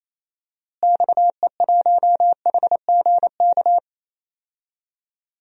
Morse code, heard as a keyed tone, about 28 wpm, 700 Hz. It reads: XE1HGK
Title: DXCC-100 Call Signs - 6 Characters - Repeated 3x - 28wpm